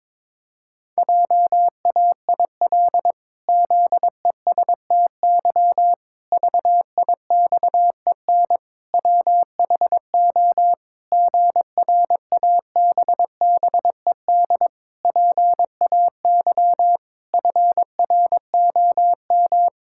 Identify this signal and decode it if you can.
Morse code, heard as a keyed tone, about 22 wpm, 700 Hz. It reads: JAIL ZESTY VIXEN WHO GRABBED PAY FROM
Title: Pangram Challenge v2 - 22wpm